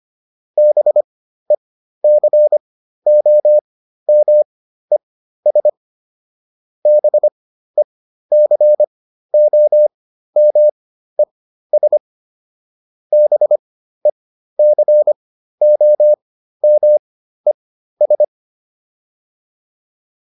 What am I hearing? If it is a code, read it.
Morse code, heard as a keyed tone, about 25 wpm, spaced as for 13 wpm, 600 Hz. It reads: BECOMES BECOMES BECOMES